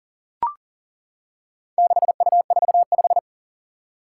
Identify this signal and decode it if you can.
Morse code, heard as a keyed tone, about 37 wpm, 700 Hz. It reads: 6U45
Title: Four Challenging Letter-Number Combinations 40wpm